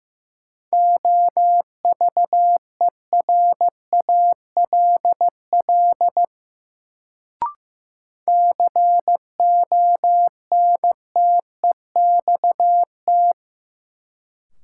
Morse code, heard as a keyed tone, about 15 wpm, 700 Hz. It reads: OVERALL CONTEXT
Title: Sentences from Top 1000 Words - Review Part 7 - New Word Drill - 15wpm